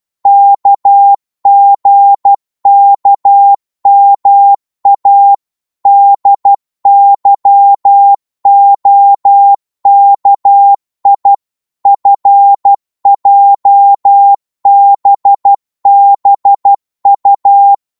Morse code, heard as a keyed tone, about 12 wpm, 800 Hz. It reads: KGKMA DYOKI FJBBU
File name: Morse / ltr-12wpm